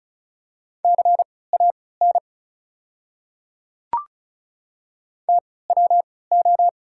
Morse code, heard as a keyed tone, about 36 wpm, spaced as for 20 wpm, 690 Hz. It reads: CAN TWO
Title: 3x Character Spacing - Top 100 Words - 35wpm